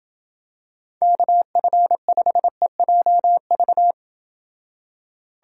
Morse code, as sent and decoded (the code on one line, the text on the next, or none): -.- ..-. ..... .--- ...-
KF5EJV